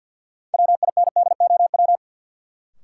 Morse code, heard as a keyed tone, about 53 wpm, 700 Hz. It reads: WINDOW
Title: Sentences from Top 1600 Words - New Word Drill Alt - 50wpm